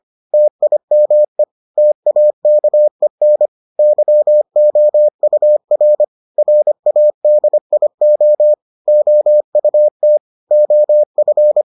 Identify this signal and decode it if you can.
Morse code, heard as a keyed tone, about 25 wpm, 600 Hz. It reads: TIME TAKEN YOUR RADIO OUT OF